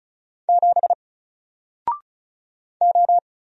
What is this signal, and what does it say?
Morse code, heard as a keyed tone, about 36 wpm, 700 Hz. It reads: 7 O
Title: Single Letter-Number Alt 35wpm